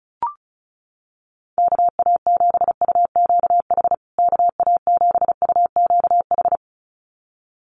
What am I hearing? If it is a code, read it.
Morse code, heard as a keyed tone, about 34 wpm, 700 Hz. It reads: KA7UQH KA7UQH
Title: USA Call Signs Repeated 2x 35wpm